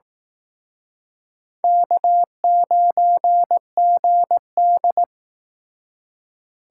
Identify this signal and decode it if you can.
Morse code, heard as a keyed tone, about 18 wpm, 700 Hz. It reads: K9GD
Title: Speed Racing - USA 1x2 Call Signs 15wpm